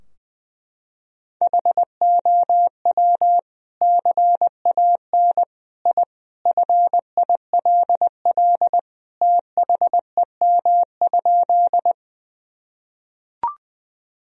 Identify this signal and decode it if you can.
Morse code, heard as a keyed tone, about 19 wpm, 700 Hz. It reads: HOW CAN I FILL THEM?
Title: Sentences from Top 1000 Words - Review Part 3 - 20wpm